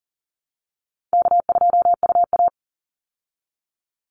Morse code, heard as a keyed tone, about 40 wpm, 700 Hz. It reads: K2UA
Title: Easy Call Signs Repeated 3x - 40wpm